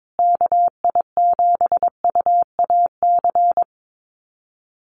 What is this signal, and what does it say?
Morse code, heard as a keyed tone, about 22 wpm, 700 Hz. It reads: KI7UAC
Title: USA Call Signs Repeated 3x 22wpm